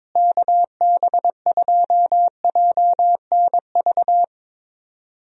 Morse code, heard as a keyed tone, about 22 wpm, 700 Hz. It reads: KB2JNV
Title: USA Call Signs - 6 Characters - Repeated 3x - 22wpm